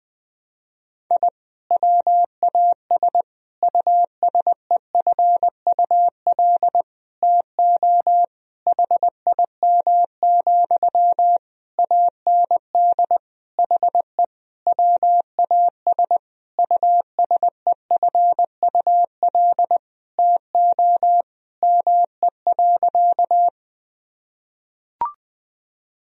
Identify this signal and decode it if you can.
Morse code, heard as a keyed tone, about 20 wpm, 700 Hz. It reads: I WAS USEFUL TO HIM, AND HE WAS USEFUL TO ME.
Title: Sentences from Top 1000 Words - Review Part 4 - 20wpm